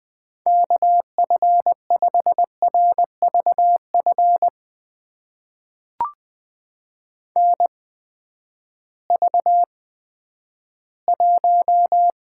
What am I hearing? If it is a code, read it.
Morse code, heard as a keyed tone, about 20 wpm, spaced as for 11 wpm, 700 Hz. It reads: KF5RVF NV1
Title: USA Call Signs - 8x Character Spacing - 20wpm